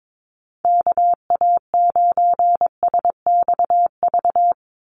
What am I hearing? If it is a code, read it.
Morse code, heard as a keyed tone, about 22 wpm, 700 Hz. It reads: KA9SXV
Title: Call Signs - 22wpm